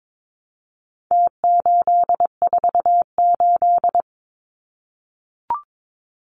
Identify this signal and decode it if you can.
Morse code, heard as a keyed tone, about 22 wpm, 700 Hz. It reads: T848
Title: Numbers - Contest Style - 4-Digit 22wpm